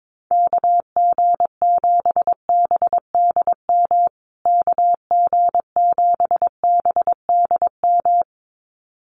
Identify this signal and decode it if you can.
Morse code, heard as a keyed tone, about 22 wpm, 700 Hz. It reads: KG7BDM KG7BDM
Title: USA Call Signs Repeated 2x 22wpm